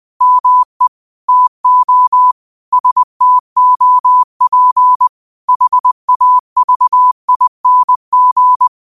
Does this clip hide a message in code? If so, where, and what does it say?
Yes: Morse code from 0.2 to 8.7 s, heard as a keyed tone, about 20 wpm, 1000 Hz. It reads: ME TO STOP HAVING